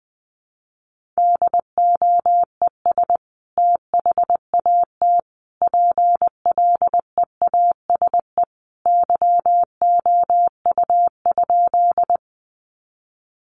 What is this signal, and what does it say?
Morse code, heard as a keyed tone, about 20 wpm, 700 Hz. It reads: DOES THAT PLEASE YOU?
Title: Sentences from Top 1500 Words 20wpm